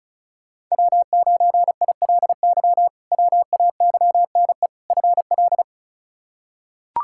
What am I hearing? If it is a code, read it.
Morse code, heard as a keyed tone, about 34 wpm, 700 Hz. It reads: W9ILY WAYNE FL
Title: K1USN SST QSO Practice - 35wpm